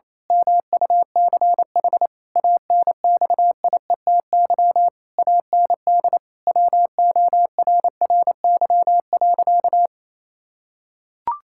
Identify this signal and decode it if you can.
Morse code, heard as a keyed tone, about 28 wpm, 700 Hz. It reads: MUCH ANXIETY AND WORRY.
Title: Pangram Challenge v2 - 28wpm